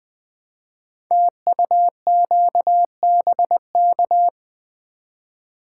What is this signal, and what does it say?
Morse code, heard as a keyed tone, about 20 wpm, 700 Hz. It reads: TUQBK
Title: Code Groups - 5 Letters - 20wpm